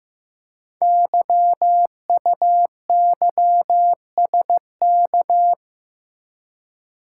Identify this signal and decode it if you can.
Morse code, heard as a keyed tone, about 15 wpm, 700 Hz. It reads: YUYSK
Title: Code Groups - 5 Letters - 15wpm